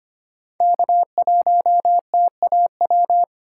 Morse code, heard as a keyed tone, about 25 wpm, 700 Hz. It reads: K1TAW
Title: USA Call Signs - 3x Character Spacing - 25wpm